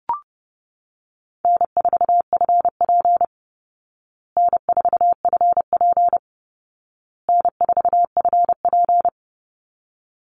Morse code, heard as a keyed tone, about 30 wpm, 700 Hz. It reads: N4FP N4FP N4FP
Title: Call Signs Repeated 2x - 30wpm